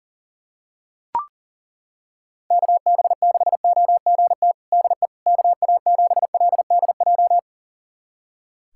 Morse code, heard as a keyed tone, about 37 wpm, 700 Hz. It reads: KB6OGT DE KA7LDJ
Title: USA Callsign DE Callsign - 6 Characters - 40wpm